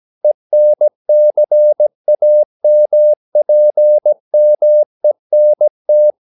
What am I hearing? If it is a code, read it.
Morse code, heard as a keyed tone, about 17 wpm, 600 Hz. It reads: ENCAMPMENT
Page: Romanen Never af Ken Follet på engelsk.